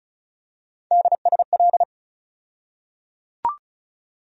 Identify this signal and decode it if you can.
Morse code, heard as a keyed tone, about 36 wpm, 700 Hz. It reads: DSL
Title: USA Call Sign Postfixes - 3 Letters 35wpm